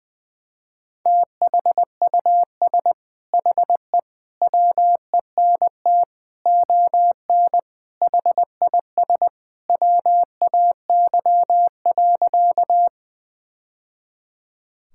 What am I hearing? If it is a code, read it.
Morse code, heard as a keyed tone, about 20 wpm, 700 Hz. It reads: THUS HE WENT ON HIS WAY.
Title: Sentences from Top 1000 Words - Review Part 5 - 20wpm